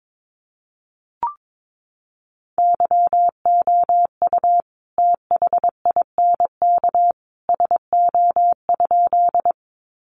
Morse code, heard as a keyed tone, about 22 wpm, 700 Hz. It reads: YOU THINK SO?
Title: Sentences from Top 200 Words 22wpm